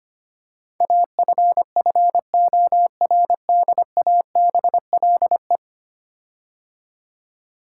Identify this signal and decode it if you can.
Morse code, heard as a keyed tone, about 25 wpm, 700 Hz. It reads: AFFORDABLE
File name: 10 Letter Words - First 3 Letters Given - 25wpm